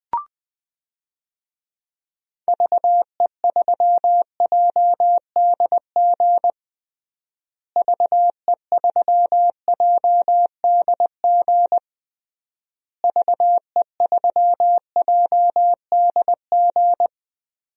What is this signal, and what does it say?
Morse code, heard as a keyed tone, about 20 wpm, 700 Hz. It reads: VE3JDG VE3JDG VE3JDG
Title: DXCC-100 Call Signs - 6 Characters - Repeated 3x - 20wpm